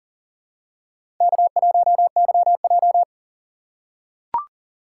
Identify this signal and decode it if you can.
Morse code, heard as a keyed tone, about 40 wpm, 700 Hz. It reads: K1YJ
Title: USA Call Signs - 4 Characters - 6x Character Spacing - 40wpm